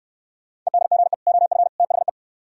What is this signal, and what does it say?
Morse code, heard as a keyed tone, about 63 wpm, 700 Hz. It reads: EDGE OF THE